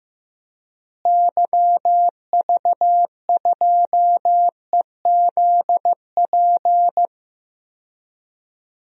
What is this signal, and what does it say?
Morse code, heard as a keyed tone, about 15 wpm, 700 Hz. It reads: YV2EZP
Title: DXCC-100 Call Signs - 6 Characters - Repeated 3x - 15wpm